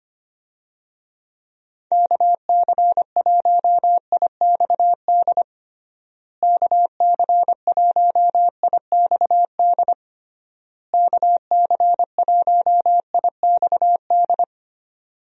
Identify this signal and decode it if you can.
Morse code, heard as a keyed tone, about 25 wpm, 700 Hz. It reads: KC1IXD KC1IXD KC1IXD
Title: USA Call Signs - 6 Characters - Repeated 3x - 25wpm